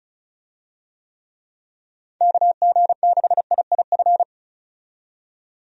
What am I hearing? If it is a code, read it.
Morse code, heard as a keyed tone, about 36 wpm, 700 Hz. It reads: KG6IIF
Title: USA Call Signs - 6 Characters - 35wpm